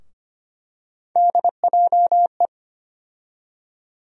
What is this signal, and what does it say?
Morse code, heard as a keyed tone, about 25 wpm, 700 Hz. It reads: DJE
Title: USA Call Sign Postfixes - 3 Letters 25wpm